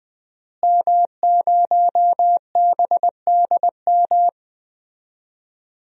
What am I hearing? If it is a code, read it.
Morse code, heard as a keyed tone, about 20 wpm, 700 Hz. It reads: M0BDM
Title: DXCC-100 Call Signs - 5 Characters - Repeated 3x - 20wpm